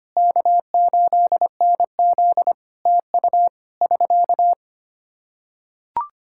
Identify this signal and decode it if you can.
Morse code, heard as a keyed tone, about 25 wpm, 700 Hz. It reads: K8NZ TU <SK>
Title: Practice QSO - Exchange 4 - Ending - 25wpm